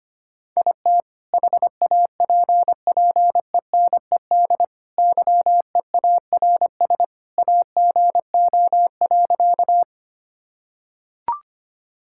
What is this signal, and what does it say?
Morse code, heard as a keyed tone, about 25 wpm, 700 Hz. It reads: IT HAPPENED YEARS AGO.
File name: practice / 25wpm / Sentences with 2-Word Phrases - 25wpm